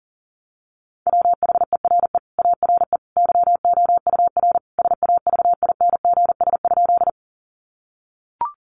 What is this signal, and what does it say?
Morse code, heard as a keyed tone, about 40 wpm, 700 Hz. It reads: WHERE ARE YOUR SAVINGS?
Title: Sentences from Top 1200 Words 40wpm